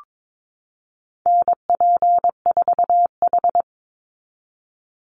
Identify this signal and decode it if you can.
Morse code, heard as a keyed tone, about 22 wpm, 700 Hz. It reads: NP4H
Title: Call Signs Repeated 3x - 22wpm